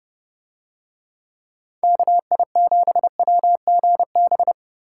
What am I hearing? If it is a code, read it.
Morse code, heard as a keyed tone, about 28 wpm, 700 Hz. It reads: KI7WGB